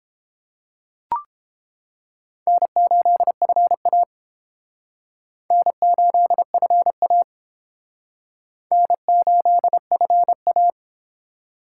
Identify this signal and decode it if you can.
Morse code, heard as a keyed tone, about 28 wpm, 700 Hz. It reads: N8FA N8FA N8FA